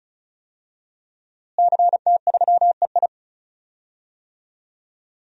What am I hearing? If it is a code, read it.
Morse code, heard as a keyed tone, about 34 wpm, 700 Hz. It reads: CT3EI
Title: DXCC-100 Call Signs - 5 Characters - Repeated 2x - 35wpm